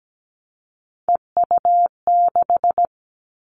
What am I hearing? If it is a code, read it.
Morse code, heard as a keyed tone, about 17 wpm, 700 Hz. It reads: EU6
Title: Callsign Prefixes for Top 100 DXCC - 17wpm